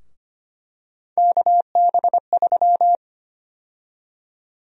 Morse code, heard as a keyed tone, about 25 wpm, 700 Hz. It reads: KB3
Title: USA Call Sign Prefixes - Repeated 2x - 25wpm